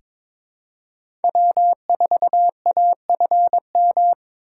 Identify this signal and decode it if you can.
Morse code, heard as a keyed tone, about 22 wpm, 700 Hz. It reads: W4AFM
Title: USA Call Signs - 5 Characters - 2x Character Spacing - 22wpm